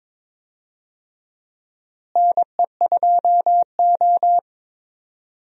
Morse code, heard as a keyed tone, about 22 wpm, 700 Hz. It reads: NE2O
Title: USA Call Signs - 4 Characters - Repeated 3x - 22wpm